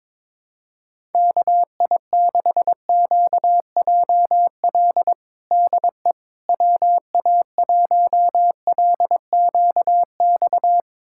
Morse code, heard as a keyed tone, about 21 wpm, 700 Hz. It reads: KI6QJL DE WA1LQX
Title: USA Callsign DE Callsign - 6 Characters - 22wpm